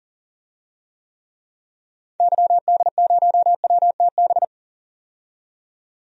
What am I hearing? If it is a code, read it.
Morse code, heard as a keyed tone, about 40 wpm, 700 Hz. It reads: YD0WTB
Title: DXCC-100 Call Signs - 6 Characters - 40wpm